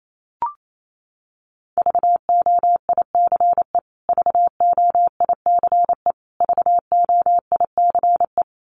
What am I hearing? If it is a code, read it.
Morse code, heard as a keyed tone, about 28 wpm, 700 Hz. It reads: VOICE VOICE VOICE
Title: Top 500 Words Repeated 28wpm